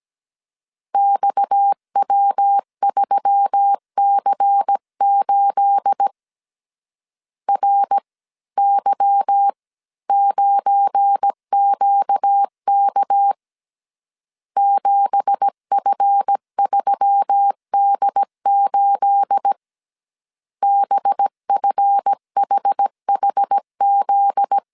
Morse code, heard as a keyed tone, about 17 wpm, 800 Hz. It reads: XW3C8 RY9QK 7F3D8 BFHHZ
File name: CW / DET / 1 9 1500C S C 16wpm 2909